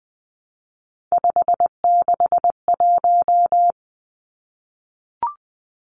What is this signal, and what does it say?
Morse code, heard as a keyed tone, about 20 wpm, 700 Hz. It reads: 561
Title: Numbers - 3-Digit 20wpm